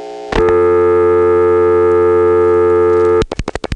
Dial tone (North America), 350 + 440 Hz, with noise.
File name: phonetrips / PrinceWmCoVA5HQ